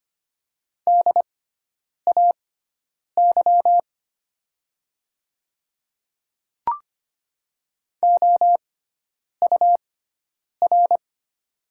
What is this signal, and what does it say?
Morse code, heard as a keyed tone, about 25 wpm, spaced as for 9 wpm, 690 Hz. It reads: DAY OUR
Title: 6x Character Spacing - Top 200 Words - 25wpm